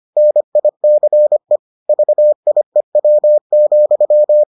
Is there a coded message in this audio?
Yes: Morse code from 0.2 to 4.4 s, heard as a keyed tone, about 25 wpm, 600 Hz. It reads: NICE VIEW,